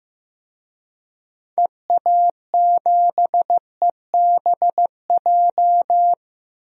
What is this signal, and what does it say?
Morse code, heard as a keyed tone, about 15 wpm, 700 Hz. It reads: EA7EBJ
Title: DXCC-100 Call Signs - 6 Characters - Repeated 2x - 15wpm